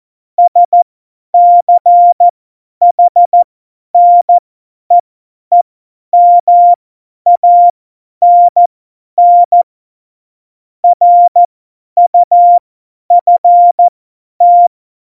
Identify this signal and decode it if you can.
Morse code, heard as a keyed tone, about 14 wpm, spaced as for 10 wpm, 700 Hz. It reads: SCHNEEMANN RUFT
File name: Morsekurs / Witze / Witz 05. September 17, Tempo 50